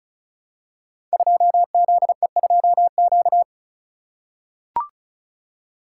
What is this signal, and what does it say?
Morse code, heard as a keyed tone, about 34 wpm, 700 Hz. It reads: 2ZE2Q
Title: Code Groups - 5 Letter:Numbers - 35wpm